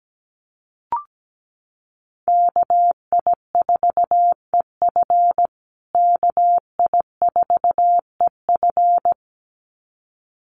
Morse code, heard as a keyed tone, about 17 wpm, 700 Hz. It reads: KI4EF KI4EF
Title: USA Call Signs Repeated 2x 17wpm